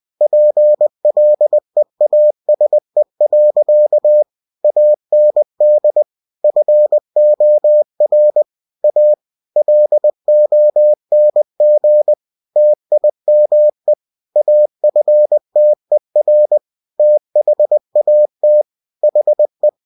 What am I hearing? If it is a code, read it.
Morse code, heard as a keyed tone, about 20 wpm, 600 Hz. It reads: PLEASE. AND FOR A LONG TIME AFTER THAT HE
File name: cw-books / winnie-the-pooh / winnie-20wpm-page-0015